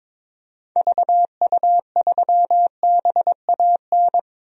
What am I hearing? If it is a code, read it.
Morse code, heard as a keyed tone, about 22 wpm, 700 Hz. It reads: VU3BAN
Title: DXCC-100 Call Signs - 6 Characters - 22wpm